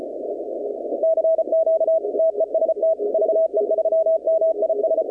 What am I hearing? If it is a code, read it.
Morse code, heard as a keyed tone, about 33 wpm, 660 Hz, with noise. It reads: CQ TEST VE3MIS
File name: contests / 2019-cw-wpx-kt5j-k5pi- -2330 0-21